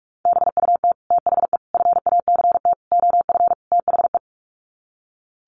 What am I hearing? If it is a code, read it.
Morse code, heard as a keyed tone, about 44 wpm, 700 Hz. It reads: BUT THE FACT OF THE
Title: Sets of 5 Words Encore2 45wpm